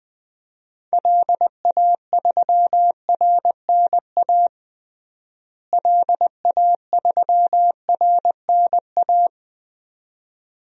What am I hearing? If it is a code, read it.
Morse code, heard as a keyed tone, about 20 wpm, 700 Hz. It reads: LA3RNA LA3RNA
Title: DXCC-100 Call Signs - 6 Characters - Repeated 2x - 20wpm